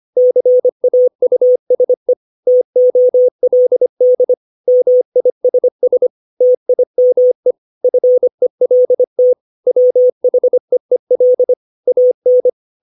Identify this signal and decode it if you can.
Morse code, heard as a keyed tone, about 25 wpm, 500 Hz. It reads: CAUSE TOLD MISS TIME FELT WHEEL AN
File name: 632 words / 632 Words 25WPM